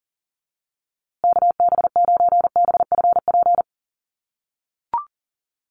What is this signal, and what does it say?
Morse code, heard as a keyed tone, about 40 wpm, 700 Hz. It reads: KB9BFP
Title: USA Call Signs Repeated 3x 40wpm